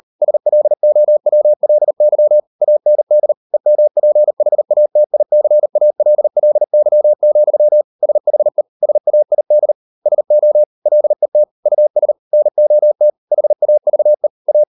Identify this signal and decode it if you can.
Morse code, heard as a keyed tone, about 38 wpm, 600 Hz. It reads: SLOWLY AND EMPHATICALLY, SHE SAID SO LET US NOT HAVE A